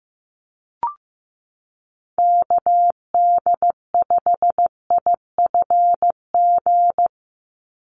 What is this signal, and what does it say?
Morse code, heard as a keyed tone, about 15 wpm, 700 Hz. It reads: KD5IFG
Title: USA Call Signs 15wpm